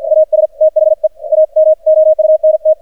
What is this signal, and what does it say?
Morse code, heard as a keyed tone, about 48 wpm, 620 Hz. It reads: AS THE AMOUNT
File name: archived / CRSnet